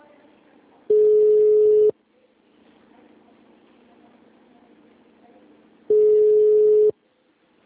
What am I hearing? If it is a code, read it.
Ringback tone (Europe, 425 Hz).
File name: out- -2009- - - .76